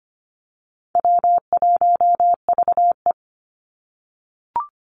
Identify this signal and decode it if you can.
Morse code, heard as a keyed tone, about 25 wpm, 700 Hz. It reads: W1VE
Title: Call Signs - 25wpm